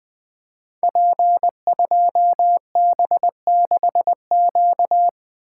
Morse code, heard as a keyed tone, about 20 wpm, 700 Hz. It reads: P2B6Q
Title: Code Groups - 5 Letter:Numbers - 20wpm